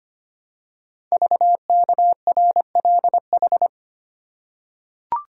Morse code, heard as a keyed tone, about 25 wpm, 700 Hz. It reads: VKRLH
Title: Code Groups - 5 Letters - 25wpm